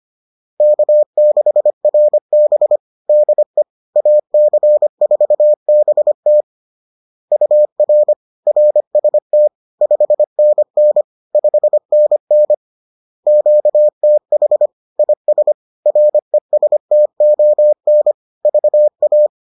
Morse code, heard as a keyed tone, about 25 wpm, 600 Hz. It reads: K6RB DE AC4BT UR RST 5NN 5NN QTH IS RESTON VA